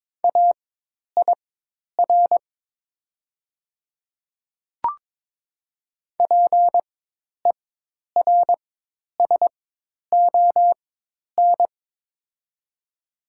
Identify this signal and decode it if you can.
Morse code, heard as a keyed tone, about 22 wpm, spaced as for 10 wpm, 700 Hz. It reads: AIR PERSON
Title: 4x Character Spacing - Top 500 Words - 22wpm